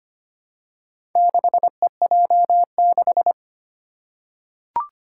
Morse code, heard as a keyed tone, about 25 wpm, 700 Hz. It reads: 6EJ6
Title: Four Challenging Letter-Number Combinations 25wpm